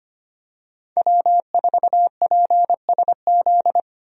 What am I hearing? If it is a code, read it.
Morse code, heard as a keyed tone, about 25 wpm, 700 Hz. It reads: W4PSZ
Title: USA Call Signs - 5 Characters - Repeated 3x - 25wpm